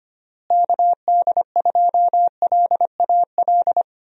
Morse code, heard as a keyed tone, about 25 wpm, 700 Hz. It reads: KD2LAL
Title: USA Call Signs - 6 Characters - Repeated 2x - 25wpm